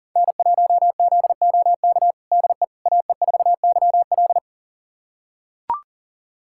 Morse code, heard as a keyed tone, about 40 wpm, 700 Hz. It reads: N1ZOK DE AE4YL
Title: USA Callsign DE Callsign - 5 Characters - 40wpm